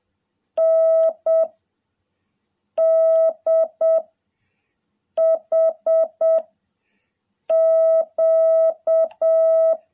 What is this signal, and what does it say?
Morse code, heard as a keyed tone, about 7 wpm, 650 Hz. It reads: N D H Q
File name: jh8cbh / 2021cw